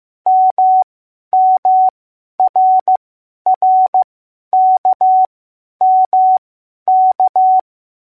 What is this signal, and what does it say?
Morse code, heard as a keyed tone, about 15 wpm, spaced as for 10 wpm, 750 Hz. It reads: MMRRKMK